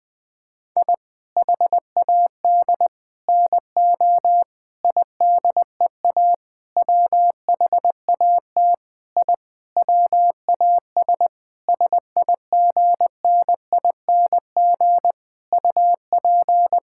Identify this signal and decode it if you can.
Morse code, heard as a keyed tone, about 20 wpm, 700 Hz. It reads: I HAD NO IDEA WHAT I WAS SIGNING UP
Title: Sentences with 7-Word Phrases - Spoken Beginning - 20wpm